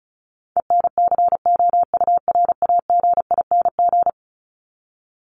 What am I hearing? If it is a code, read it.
Morse code, heard as a keyed tone, about 34 wpm, 700 Hz. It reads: ENCOURAGING
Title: Top 1751-2000 Words 35wpm